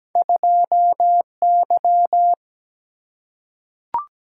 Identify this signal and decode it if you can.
Morse code, heard as a keyed tone, about 17 wpm, 700 Hz. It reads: 2Y
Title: Single-Letter Single-Number 17wpm